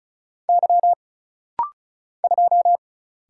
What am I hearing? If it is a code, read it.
Morse code, heard as a keyed tone, about 36 wpm, 700 Hz. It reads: Y 2
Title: Single Letter-Number - Rapid-Fire 35wpm